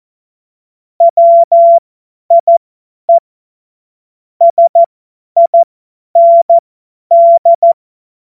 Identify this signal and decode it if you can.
Morse code, heard as a keyed tone, about 14 wpm, spaced as for 10 wpm, 680 Hz. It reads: WIE SIND
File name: Morsekurs / Witze / Witz 27. Januar 2026 50 F70